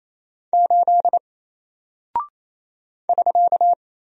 Morse code, heard as a keyed tone, about 28 wpm, 700 Hz. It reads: 8 <SK>
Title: Single Character (Letter:Number:Punctuation:Prosign) Alt 28wpm